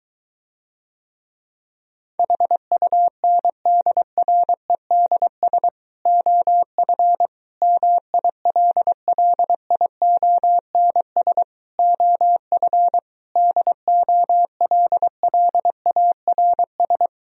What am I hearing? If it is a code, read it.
Morse code, heard as a keyed tone, about 23 wpm, 700 Hz. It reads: HUNDREDS OF MILLIONS OF DOLLARS